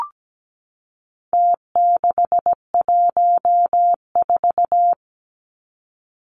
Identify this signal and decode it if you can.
Morse code, heard as a keyed tone, about 17 wpm, 700 Hz. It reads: T614
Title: Numbers - Contest Style - 4-Digit 17wpm